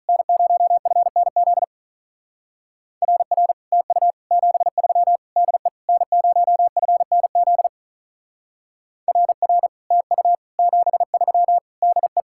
Morse code, heard as a keyed tone, about 43 wpm, 700 Hz. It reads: N0FNZ RR TU 73 DE N0FNZ RR TU 73 DE
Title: Speed-Racing Common QSO Phrases - 35wpm